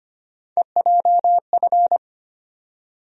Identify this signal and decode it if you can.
Morse code, heard as a keyed tone, about 25 wpm, 700 Hz. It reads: EJF
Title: USA Call Sign Postfixes - 3 Letters 25wpm